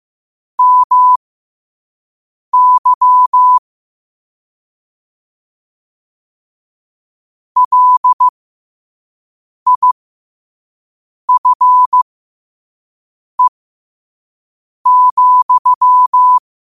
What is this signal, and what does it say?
Morse code, heard as a keyed tone, about 15 wpm, spaced as for 5 wpm, 1000 Hz. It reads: MY LIFE,
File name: qotdfiles / 5wpm